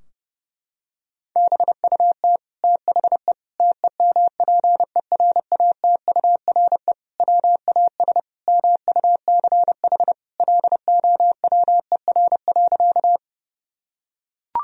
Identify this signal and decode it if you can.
Morse code, heard as a keyed tone, about 32 wpm, 700 Hz. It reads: BUT THE TEMPERATURE WAS MUCH LOWER.
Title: Sentences from Top 1000 Words - Review Part 4 - 30wpm